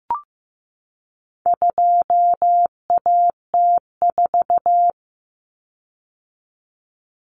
Morse code, heard as a keyed tone, about 15 wpm, 700 Hz. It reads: 2AT4
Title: Common Cut Numbers - 4-Digit 15wpm